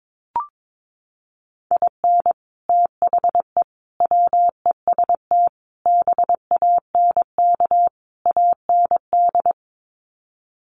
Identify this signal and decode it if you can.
Morse code, heard as a keyed tone, about 22 wpm, 700 Hz. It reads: IN THE WEST BANK AND
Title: Sets of 5 Words Encore2 22wpm